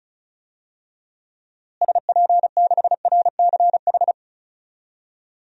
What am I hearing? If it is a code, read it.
Morse code, heard as a keyed tone, about 36 wpm, 700 Hz. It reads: SP6RCH
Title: DXCC-100 Call Signs - 6 Characters - Repeated 3x - 35wpm